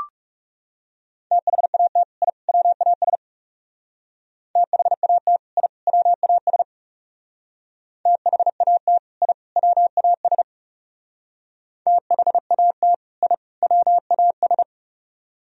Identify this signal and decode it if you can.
Morse code, heard as a keyed tone, about 35 wpm, 700 Hz. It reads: THAT I WAS THAT I WAS THAT I WAS THAT I WAS
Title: Speed-Racing - Sets of 3 Words 30wpm